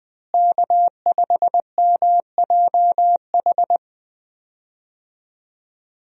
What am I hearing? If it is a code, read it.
Morse code, heard as a keyed tone, about 20 wpm, 700 Hz. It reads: K5MJH
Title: USA Call Signs - 5 Characters - Repeated 2x - 20wpm